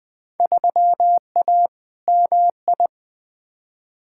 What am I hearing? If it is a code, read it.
Morse code, heard as a keyed tone, about 21 wpm, 700 Hz. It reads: 3A MI
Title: Field Day Exchanges - 20wpm